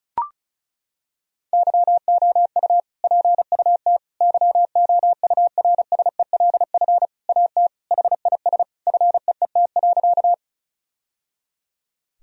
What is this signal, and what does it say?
Morse code, heard as a keyed tone, about 36 wpm, 700 Hz. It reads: YOU PUT YOURSELF AT HIS FEET.
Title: Sentences from Top 1000 Words - Review Part 1 - 35wpm